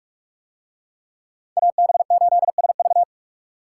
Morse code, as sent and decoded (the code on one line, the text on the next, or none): .- -... ---.. ... ...-
AB8SV